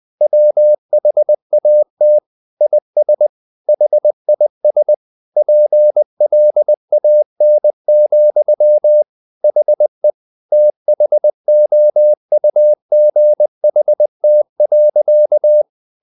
Morse code, heard as a keyed tone, about 20 wpm, 600 Hz. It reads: WHAT IS HIS PLAN, HE THOUGHT.
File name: the old man and the sea / the old man and the sea part 1 20wpm